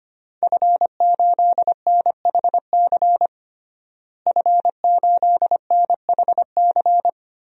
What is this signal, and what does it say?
Morse code, heard as a keyed tone, about 25 wpm, 700 Hz. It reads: F8NHC F8NHC
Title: DXCC-100 Call Signs - 5 Characters - Repeated 3x - 25wpm